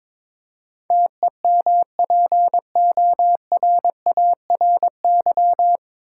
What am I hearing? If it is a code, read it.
Morse code, heard as a keyed tone, about 22 wpm, 700 Hz. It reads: TEMPORARY
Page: Morse Code Ninja practice 22wpm